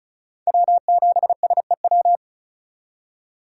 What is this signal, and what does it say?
Morse code, heard as a keyed tone, about 34 wpm, 700 Hz. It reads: W7SEW
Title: USA Call Signs - 5 Characters - 4x Character Spacing - 35wpm